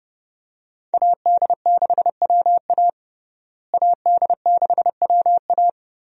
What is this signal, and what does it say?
Morse code, heard as a keyed tone, about 29 wpm, 700 Hz. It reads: AD6WA AD6WA
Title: USA Call Signs - 5 Characters - Repeated 3x - 30wpm